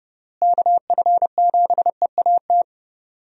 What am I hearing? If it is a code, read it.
Morse code, heard as a keyed tone, about 32 wpm, 700 Hz. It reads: KF7EAT
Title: USA Call Signs - 6 Characters - Repeated 3x - 30wpm